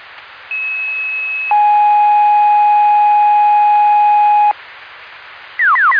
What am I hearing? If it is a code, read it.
Two-tone page, heard as a one-second tone, then a three-second tone, about 2600 then 830 Hz. It is heard from 0.5 to 4.5 s.